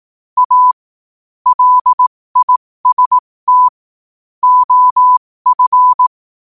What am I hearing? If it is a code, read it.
Morse code, heard as a keyed tone, about 18 wpm, spaced as for 15 wpm, 1000 Hz. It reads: A LIST OF
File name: qotdfiles / 15wpm